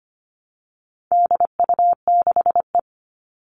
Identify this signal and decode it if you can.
Morse code, heard as a keyed tone, about 25 wpm, 700 Hz. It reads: DU6E
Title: All Cut Numbers - 4-Digit 25wpm